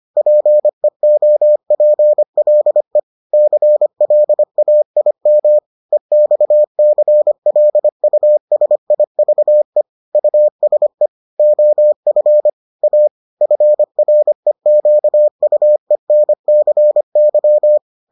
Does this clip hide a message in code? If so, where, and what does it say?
Yes: Morse code from 0.2 to 17.8 s, heard as a keyed tone, about 25 wpm, 600 Hz. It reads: PEOPLE CLAIM EXCLUSIVE USE OF A FREQUENCY